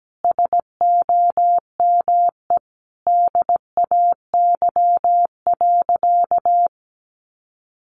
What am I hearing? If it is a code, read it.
Morse code, heard as a keyed tone, about 17 wpm, 700 Hz. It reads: SOME DAY.
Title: Sentences from Top 900 Words 17wpm